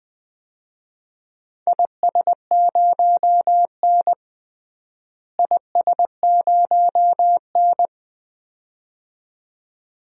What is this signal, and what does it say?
Morse code, heard as a keyed tone, about 20 wpm, 700 Hz. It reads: IS0N IS0N
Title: DXCC-100 Call Signs - 4 Characters - Repeated 2x - 20wpm